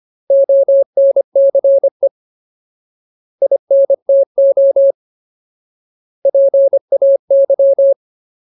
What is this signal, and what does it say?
Morse code, heard as a keyed tone, about 25 wpm, 550 Hz. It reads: ONCE INTO PAY
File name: Ultimate Word List / 25WPM Ultimate